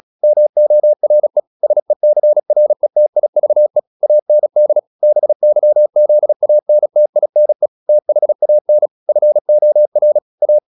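Morse code, heard as a keyed tone, about 36 wpm, 600 Hz. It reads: MORE SECRETIVE AND BYZANTINE THAN FOR A
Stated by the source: Morse practice at 36 wpm, text of Ken Follett's novel Never, from CW med Gnister.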